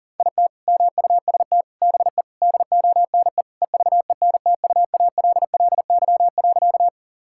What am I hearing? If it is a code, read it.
Morse code, heard as a keyed tone, about 37 wpm, 700 Hz. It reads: IT MUST BE DONE EVENTUALLY.
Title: Sentences from Top 1000 Words - Review Part 6 - 40wpm